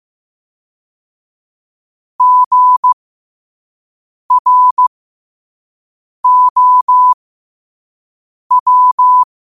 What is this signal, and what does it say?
Morse code, heard as a keyed tone, about 15 wpm, spaced as for 5 wpm, 1000 Hz. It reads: GROW
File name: qotdfiles / 5wpm